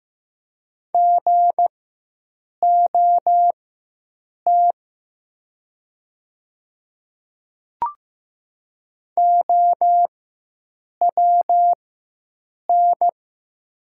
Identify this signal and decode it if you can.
Morse code, heard as a keyed tone, about 15 wpm, spaced as for 7 wpm, 700 Hz. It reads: GOT OWN
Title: 4x Character Spacing - Top 300 Words - 15wpm